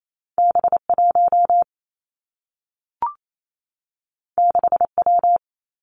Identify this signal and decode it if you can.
Morse code, heard as a keyed tone, about 28 wpm, 700 Hz. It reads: B 1 6 W
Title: Two Challenging Letter-Number Combinations 28wpm